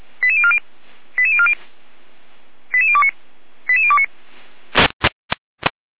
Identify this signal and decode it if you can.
Five-tone selective call (ZVEI), heard as quick runs of five tones, tones of about 70 ms. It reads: 88040 88040 88029 88029